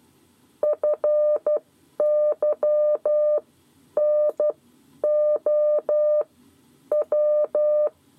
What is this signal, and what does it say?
Morse code, heard as a keyed tone, about 12 wpm, spaced as for 9 wpm, 600 Hz. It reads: FYNOW
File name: 7 Lezione cw 2